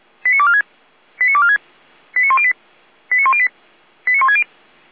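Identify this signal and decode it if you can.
Five-tone selective call (ZVEI), heard as quick runs of five tones, tones of about 70 ms. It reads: 89246 89246 89108 89108 89160
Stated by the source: ZVEI five-tone selective call received on SDR.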